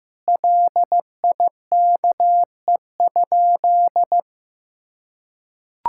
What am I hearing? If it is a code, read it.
Morse code, heard as a keyed tone, about 15 wpm, 700 Hz. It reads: LIKE?
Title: Sentences from Top 1000 Words - Review Part 6 - 15wpm